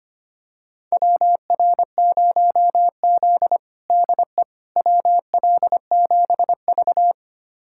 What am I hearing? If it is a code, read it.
Morse code, heard as a keyed tone, about 25 wpm, 700 Hz. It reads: WR0Z DE WL7V
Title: USA Callsign DE Callsign - 4 Characters - 25wpm